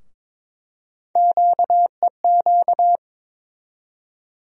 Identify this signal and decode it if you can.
Morse code, heard as a keyed tone, about 22 wpm, 700 Hz. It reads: QEQ
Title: USA Call Sign Postfixes - 3 Letters 22wpm